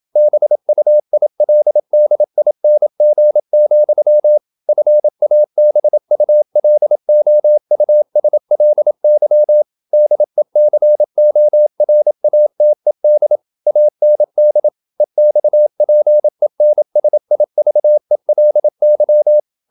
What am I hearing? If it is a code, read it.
Morse code, heard as a keyed tone, about 27 wpm, 600 Hz. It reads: BUILDING, FABULOUSLY DECORATED AND EXPENSIVELY
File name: Never 0027wpm